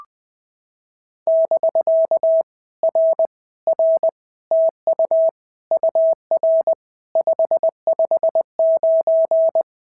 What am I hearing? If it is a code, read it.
Morse code, heard as a keyed tone, about 20 wpm, 650 Hz. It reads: <BK> R R TU UR 559
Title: POTA QSO Practice - 20wpm